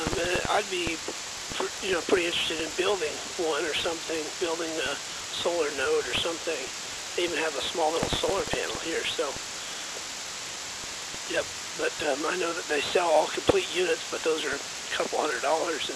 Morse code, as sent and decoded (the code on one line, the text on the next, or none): none